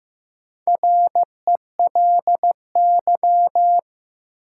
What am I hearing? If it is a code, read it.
Morse code, heard as a keyed tone, about 15 wpm, 700 Hz. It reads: RELY
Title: Sentences from Top 1200 Words - New Word Drill Alt - 15wpm